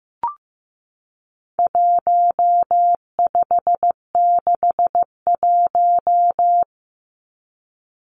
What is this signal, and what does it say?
Morse code, heard as a keyed tone, about 15 wpm, 700 Hz. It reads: 1561
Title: Numbers - 4-Digit 15wpm